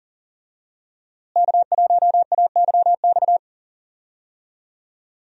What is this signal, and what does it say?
Morse code, heard as a keyed tone, about 40 wpm, 700 Hz. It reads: K1AYX
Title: USA Call Signs - 5 Characters - Repeated 3x - 40wpm